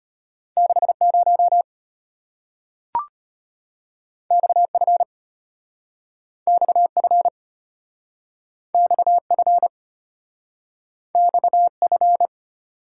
Morse code, heard as a keyed tone, about 34 wpm, 700 Hz. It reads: 60 XF XF XF XF